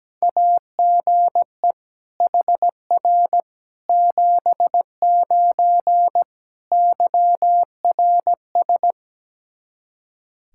Morse code, heard as a keyed tone, about 17 wpm, 700 Hz. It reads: AGE HR 79 YRS
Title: Practice QSO - Exchange 3 - More Info - 17wpm